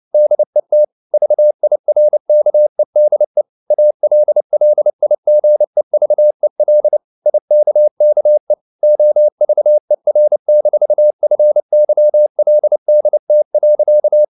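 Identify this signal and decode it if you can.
Morse code, heard as a keyed tone, about 29 wpm, 600 Hz. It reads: DET VIRKEDE ALLIGEVEL IKKE OVER-FYLDT.
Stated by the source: Fast (QRQ) Morse at 29 wpm, from CW med Gnister.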